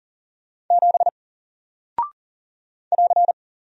Morse code, heard as a keyed tone, about 40 wpm, 700 Hz. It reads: 7 <AR>
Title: Single Character (Letter:Number:Punctuation:Prosign) Alt 40wpm